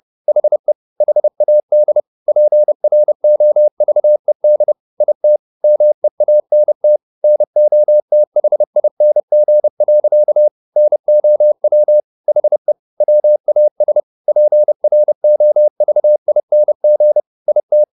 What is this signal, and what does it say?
Morse code, heard as a keyed tone, about 30 wpm, 600 Hz. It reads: HE HAD PROVED IT MEANT NOTHING. NOW HE WAS PROVING IT
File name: the old man and the sea / the old man and the sea 1 30wpm